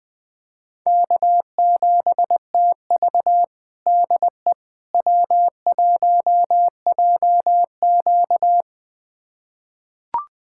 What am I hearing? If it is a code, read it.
Morse code, heard as a keyed tone, about 21 wpm, 700 Hz. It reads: K7TV DE W1JQ
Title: USA Callsign DE Callsign - 4 Characters - 20wpm